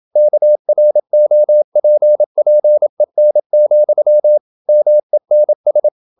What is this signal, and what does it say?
Morse code, heard as a keyed tone, about 27 wpm, 600 Hz. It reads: KROPPEN, MENS
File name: Ferie QRQ 27wpm